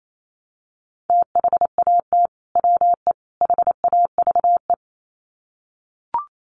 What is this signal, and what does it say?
Morse code, heard as a keyed tone, about 28 wpm, 700 Hz. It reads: THAT WE HAVE
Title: Sets of 3 Words 28wpm